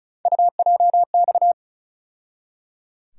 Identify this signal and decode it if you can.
Morse code, heard as a keyed tone, about 36 wpm, 700 Hz. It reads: UJX
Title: USA Call Sign Postfixes - 3 Letters 35wpm